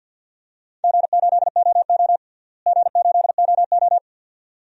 Morse code, heard as a keyed tone, about 51 wpm, 700 Hz. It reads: G8OO G8OO
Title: DXCC-100 Call Signs - 4 Characters - Repeated 2x - 50wpm